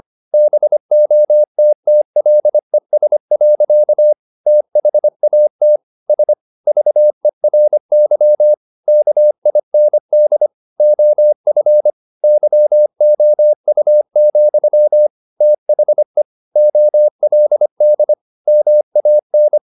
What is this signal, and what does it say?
Morse code, heard as a keyed tone, about 25 wpm, 600 Hz. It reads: BOTTLES. THAT S VERY KIND OF YOU, THE OLD MAN
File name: the old man and the sea part 1 25wpm